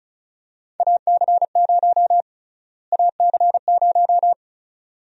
Morse code, heard as a keyed tone, about 36 wpm, 700 Hz. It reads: AC0 AC0
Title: USA Call Sign Prefixes - Repeated 2x - 35wpm